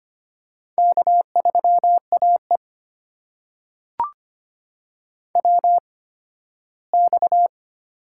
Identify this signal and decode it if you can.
Morse code, heard as a keyed tone, about 25 wpm, spaced as for 15 wpm, 700 Hz. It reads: K3AE WX
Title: USA Call Signs - 4 Characters - 8x Character Spacing - 25wpm